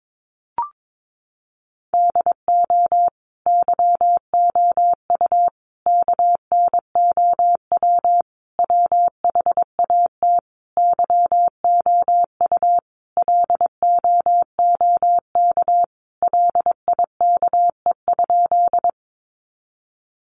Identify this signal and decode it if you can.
Morse code, heard as a keyed tone, about 22 wpm, 700 Hz. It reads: DO YOU KNOW WHAT YOU LOOK LIKE?
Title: Sentences from Top 100 Words 22wpm